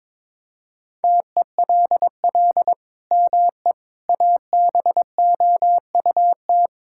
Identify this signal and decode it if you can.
Morse code, heard as a keyed tone, about 22 wpm, 700 Hz. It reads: TELL ME ABOUT
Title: Sentences from Top 1000 Words - Review Part 6 - 22wpm